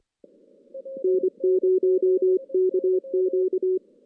Morse code, heard as a keyed tone, about 25 wpm, 370 Hz. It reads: N0KQ
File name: 2016 JAN CW NAQP / 2016 Jan naqp k5ot - 0-5